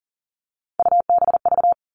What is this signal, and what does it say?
Morse code, heard as a keyed tone, about 40 wpm, 700 Hz. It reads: UBV
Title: All Cut Numbers - 3-Digit 40wpm